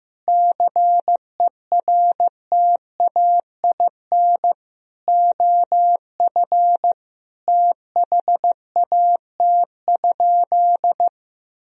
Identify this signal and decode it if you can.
Morse code, heard as a keyed tone, about 15 wpm, 700 Hz. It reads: CERTAIN OF THAT?
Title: Sentences from Top 1000 Words - Review Part 2 - 15wpm